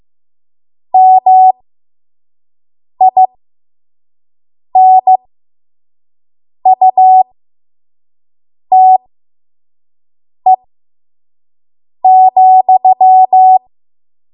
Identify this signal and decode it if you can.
Morse code, heard as a keyed tone, about 15 wpm, spaced as for 5 wpm, 750 Hz. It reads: MINUTE,
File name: Morse / 05WPM